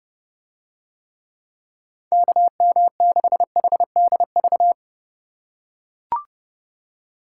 Morse code, heard as a keyed tone, about 31 wpm, 700 Hz. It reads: KM6HDV